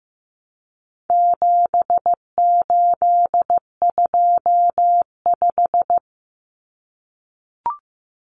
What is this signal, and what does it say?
Morse code, heard as a keyed tone, about 15 wpm, 700 Hz. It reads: 7825
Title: Numbers - 4-Digit 15wpm